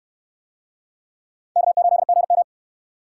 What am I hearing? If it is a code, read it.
Morse code, heard as a keyed tone, about 68 wpm, 700 Hz. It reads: K8GM